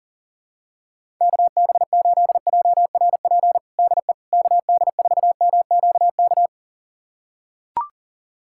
Morse code, heard as a keyed tone, about 40 wpm, 700 Hz. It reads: KB8JRP DE KD4MQK
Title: USA Callsign DE Callsign - 6 Characters - 40wpm